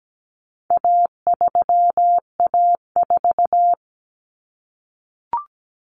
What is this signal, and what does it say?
Morse code, heard as a keyed tone, about 17 wpm, 700 Hz. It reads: A3A4
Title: Common Cut Numbers - 4-Digit 17wpm